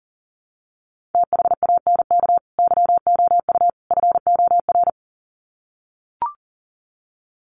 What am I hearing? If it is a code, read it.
Morse code, heard as a keyed tone, about 40 wpm, 700 Hz. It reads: THANK YOU FOR
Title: Sets of 3 Words 40wpm